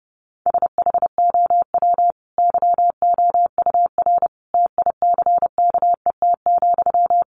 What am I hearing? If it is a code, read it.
Morse code, heard as a keyed tone, about 30 wpm, 700 Hz. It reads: SHOW YOUR TICKET,
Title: Sentences from Top 1400 Words 30wpm